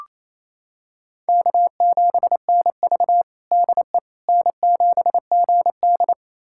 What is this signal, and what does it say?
Morse code, heard as a keyed tone, about 28 wpm, 700 Hz. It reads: K7NV DE N7GD
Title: USA Callsign DE Callsign - 4 Characters - 28wpm